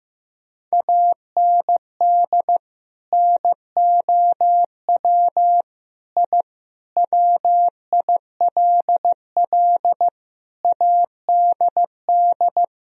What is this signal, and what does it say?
Morse code, heard as a keyed tone, about 16 wpm, 700 Hz. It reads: AND NOW I WILL ADD
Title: Sentences from Top 1000 Words - Review Part 1 - 15wpm